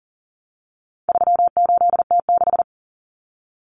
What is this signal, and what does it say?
Morse code, heard as a keyed tone, about 40 wpm, 700 Hz. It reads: 38T6
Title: Common Cut Numbers - 4-Digit 40wpm